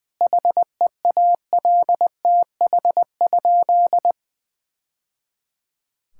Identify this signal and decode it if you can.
Morse code, heard as a keyed tone, about 20 wpm, 700 Hz. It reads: HEALTH?
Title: Sentences from Top 1000 Words - Review Part 5 - 20wpm